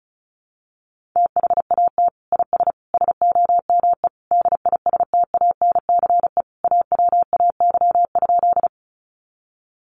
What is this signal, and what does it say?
Morse code, heard as a keyed tone, about 34 wpm, 700 Hz. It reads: THAT IS SOME DISTANCE AWAY?
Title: Sentences from Top 1200 Words 35wpm